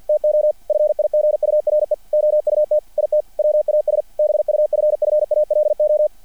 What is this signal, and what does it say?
Morse code, heard as a keyed tone, about 50 wpm, 600 Hz. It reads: TO FIGURE OUT IT WAS BUFFALO
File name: archived / CRSnet